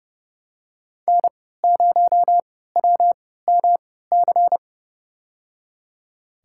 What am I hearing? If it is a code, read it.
Morse code, heard as a keyed tone, about 30 wpm, spaced as for 17 wpm, 700 Hz. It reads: N0WMC
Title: USA Call Signs - 5 Characters - 3x Character Spacing - 30wpm